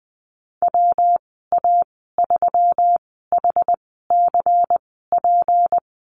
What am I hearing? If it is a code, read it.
Morse code, heard as a keyed tone, about 19 wpm, spaced as for 14 wpm, 700 Hz. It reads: WA3HCP
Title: 161 Lesson 26g - Letter g - Call Signs 20wpm